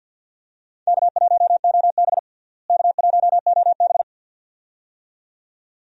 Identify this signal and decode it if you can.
Morse code, heard as a keyed tone, about 52 wpm, 700 Hz. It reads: K1OB K1OB
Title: USA Call Signs - 4 Characters - Repeated 2x - 50wpm